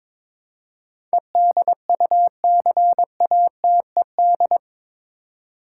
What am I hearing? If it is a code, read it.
Morse code, heard as a keyed tone, about 22 wpm, 700 Hz. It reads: EDUCATED
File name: practice / 22wpm / Eight Letter Words - 22wpm